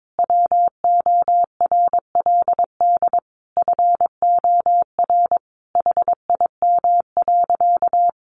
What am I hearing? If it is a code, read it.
Morse code, heard as a keyed tone, about 22 wpm, 700 Hz. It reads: WORLD FOR HIM.
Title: Sentences from Top 1300 Words 22wpm